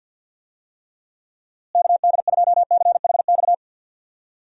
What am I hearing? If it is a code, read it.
Morse code, heard as a keyed tone, about 50 wpm, 700 Hz. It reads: KD2CHX
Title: USA Call Signs - 6 Characters - Repeated 2x - 50wpm